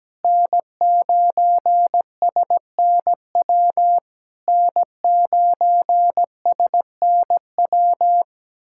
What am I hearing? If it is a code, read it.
Morse code, heard as a keyed tone, about 17 wpm, 690 Hz. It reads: N9SNW N9SNW
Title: SOTA QSO Practice 17wpm